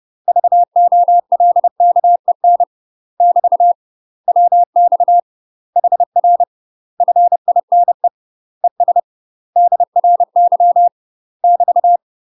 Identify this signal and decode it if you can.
Morse code, heard as a keyed tone, about 30 wpm, 700 Hz. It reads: VOLKEN <BT> WX HR FINE ES DRY <BT>
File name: UK MASTER QSOs TRAINING / 30WPM / UK MASTER QSOs 0043